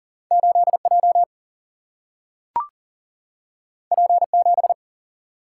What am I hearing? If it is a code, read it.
Morse code, heard as a keyed tone, about 40 wpm, 700 Hz. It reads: 8 J P 7
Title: Single-Letter Single-Number 40wpm